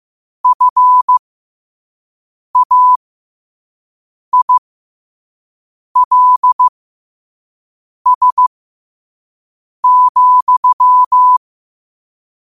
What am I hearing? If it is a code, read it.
Morse code, heard as a keyed tone, about 15 wpm, spaced as for 5 wpm, 1000 Hz. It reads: FAILS,
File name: qotdfiles / 5wpm